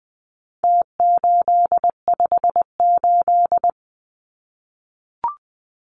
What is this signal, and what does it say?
Morse code, heard as a keyed tone, about 20 wpm, 700 Hz. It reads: T858
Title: Numbers - Contest Style - 4-Digit 20wpm